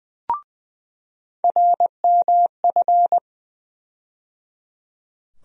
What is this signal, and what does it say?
Morse code, heard as a keyed tone, about 20 wpm, 700 Hz. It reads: RMF
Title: USA Call Sign Postfixes - 3 Letters 20wpm